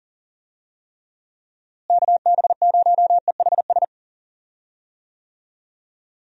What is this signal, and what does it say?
Morse code, heard as a keyed tone, about 40 wpm, 700 Hz. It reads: KB0EHS
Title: USA Call Signs - 6 Characters - Repeated 2x - 40wpm